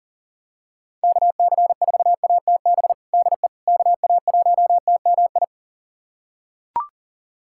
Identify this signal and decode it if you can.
Morse code, heard as a keyed tone, about 37 wpm, 700 Hz. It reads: KC4ATB DE KA1TMI
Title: USA Callsign DE Callsign - 6 Characters - 40wpm